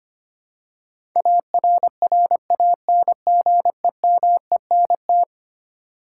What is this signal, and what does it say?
Morse code, heard as a keyed tone, about 25 wpm, 700 Hz. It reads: ARRANGEMENT
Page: morse-code-ninja-practice-25wpm